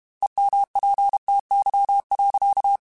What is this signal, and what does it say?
Morse code, heard as a keyed tone, about 32 wpm, 800 Hz. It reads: EMPTY.